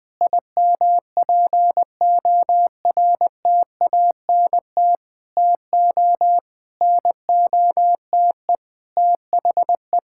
Morse code, heard as a keyed tone, about 20 wpm, 700 Hz. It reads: IMPORTANT TO NOTE THE
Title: Sentences with 5-Word Phrases - 20wpm